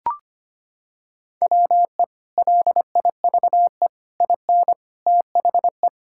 Morse code, heard as a keyed tone, about 25 wpm, 700 Hz. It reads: WE LIVE IN THE
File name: practice / 25wpm / Sentences with 3-Word Phrases - 25wpm